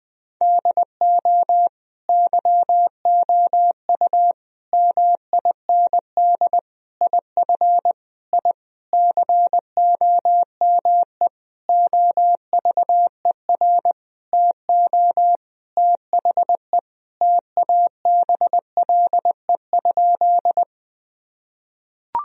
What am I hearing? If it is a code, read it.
Morse code, heard as a keyed tone, about 21 wpm, 700 Hz. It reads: DO YOU MIND IF I COME OVER TO THE TABLE?
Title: Sentences from Top 1000 Words - Review Part 2 - 20wpm